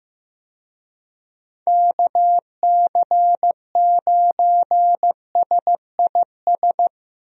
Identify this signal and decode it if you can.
Morse code, heard as a keyed tone, about 15 wpm, 700 Hz. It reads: KC9SIS
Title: USA Call Signs - 6 Characters - Repeated 2x - 15wpm